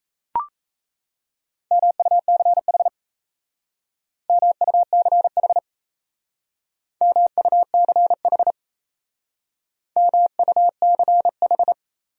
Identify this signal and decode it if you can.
Morse code, heard as a keyed tone, about 37 wpm, 700 Hz. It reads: MUCH MUCH MUCH MUCH